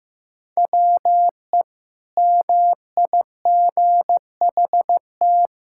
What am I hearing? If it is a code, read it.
Morse code, heard as a keyed tone, about 15 wpm, 700 Hz. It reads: WE MIGHT
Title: Sentences from Top 1000 Words - Review Part 6 - 15wpm